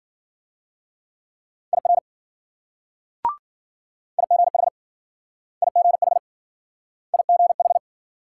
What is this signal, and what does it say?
Morse code, heard as a keyed tone, about 53 wpm, 700 Hz. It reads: IR IGH IGH IGH